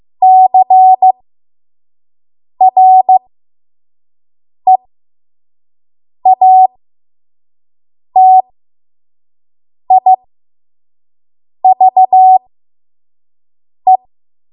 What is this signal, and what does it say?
Morse code, heard as a keyed tone, about 15 wpm, spaced as for 5 wpm, 750 Hz. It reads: CREATIVE
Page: 5 WPM Code Archive